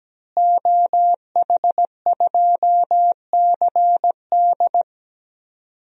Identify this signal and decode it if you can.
Morse code, heard as a keyed tone, about 17 wpm, 700 Hz. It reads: OH2CD
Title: DXCC-100 Call Signs - 5 Characters - Repeated 2x - 17wpm